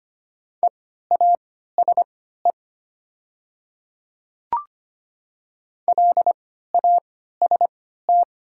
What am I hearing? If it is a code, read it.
Morse code, heard as a keyed tone, about 25 wpm, spaced as for 14 wpm, 700 Hz. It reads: EASE LAST
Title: 3x Character Spacing - Top 300 Words - 25wpm